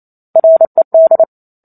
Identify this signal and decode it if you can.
Morse code, heard as a keyed tone, about 28 wpm, 650 Hz. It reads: RED